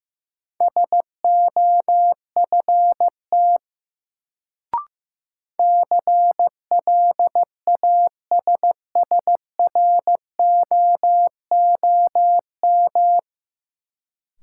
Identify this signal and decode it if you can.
Morse code, heard as a keyed tone, about 15 wpm, 700 Hz. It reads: SOFT CLASSROOM
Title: Sentences from Top 1500 Words - New Word Drill - 15wpm